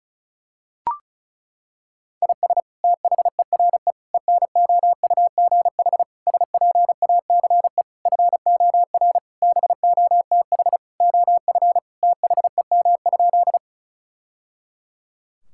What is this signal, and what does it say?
Morse code, heard as a keyed tone, about 34 wpm, 700 Hz. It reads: IS THERE ENOUGH SPACE FOR BOTH OF THEM?
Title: Sentences from Top 1000 Words - Review Part 5 - 35wpm